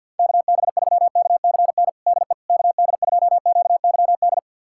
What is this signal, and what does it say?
Morse code, heard as a keyed tone, about 50 wpm, 700 Hz. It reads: KB3KCN DE KD2QYD
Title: USA Callsign DE Callsign - 6 Characters - 50wpm